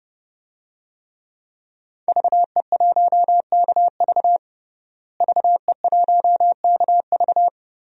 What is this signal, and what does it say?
Morse code, heard as a keyed tone, about 32 wpm, 700 Hz. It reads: VE1KV VE1KV
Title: DXCC-100 Call Signs - 5 Characters - Repeated 2x - 30wpm